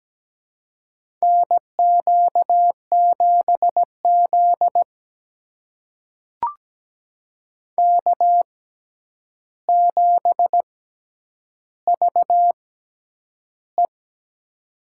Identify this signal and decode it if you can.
Morse code, heard as a keyed tone, about 17 wpm, spaced as for 9 wpm, 700 Hz. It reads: NQ7Z K7VE
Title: USA Call Signs - 4 Characters - 6x Character Spacing - 17wpm